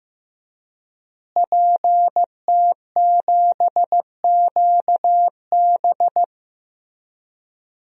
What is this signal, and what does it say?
Morse code, heard as a keyed tone, about 15 wpm, 700 Hz. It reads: PT7QB
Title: DXCC-100 Call Signs - 5 Characters - Repeated 3x - 15wpm